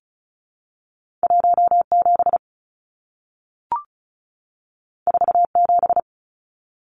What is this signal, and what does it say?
Morse code, heard as a keyed tone, about 36 wpm, 700 Hz. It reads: 17 47
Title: Numbers - 2-Digit 35wpm